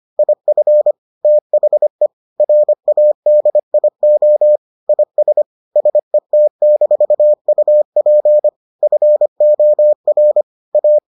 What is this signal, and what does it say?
Morse code, heard as a keyed tone, about 25 wpm, 600 Hz. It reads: IF THE RADIO IS SET-UP FOR A